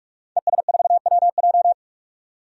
Morse code, heard as a keyed tone, about 42 wpm, 700 Hz. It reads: ES4WJ
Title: DXCC-100 Call Signs - 5 Characters - 45wpm